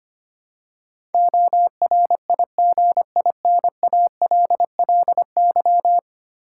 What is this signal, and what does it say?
Morse code, heard as a keyed tone, about 25 wpm, 700 Hz. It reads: ORIGINALLY
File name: Ten Letter Words - 25wpm